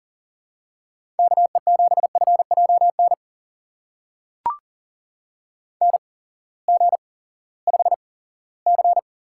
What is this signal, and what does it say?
Morse code, heard as a keyed tone, about 41 wpm, spaced as for 20 wpm, 700 Hz. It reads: KE7FJN NG5C
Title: USA Call Signs - 8x Character Spacing - 40wpm